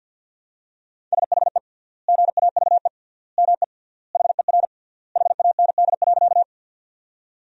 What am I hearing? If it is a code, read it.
Morse code, heard as a keyed tone, about 50 wpm, 700 Hz. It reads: SHE GAVE ME HER HAND.
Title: 3x Word-Spacing - Sentences from Top 500 Words - 50wpm